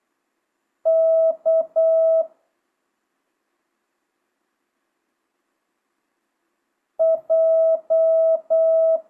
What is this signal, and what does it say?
Morse code, heard as a keyed tone, about 8 wpm, 1300 Hz. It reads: K J
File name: jh8cbh / 2025cw